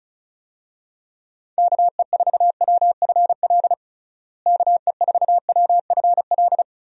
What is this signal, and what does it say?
Morse code, heard as a keyed tone, about 34 wpm, 700 Hz. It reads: KE4WFL KE4WFL
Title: USA Call Signs - 6 Characters - Repeated 3x - 35wpm